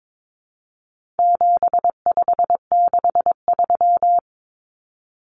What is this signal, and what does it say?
Morse code, heard as a keyed tone, about 22 wpm, 700 Hz. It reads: 7563
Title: Numbers - 4-Digit 22wpm